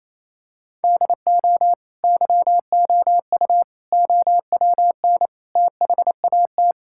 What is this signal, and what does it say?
Morse code, heard as a keyed tone, about 28 wpm, 700 Hz. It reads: DO YOU OWN THAT
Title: Sentences from Top 1000 Words - Review Part 6 - 28wpm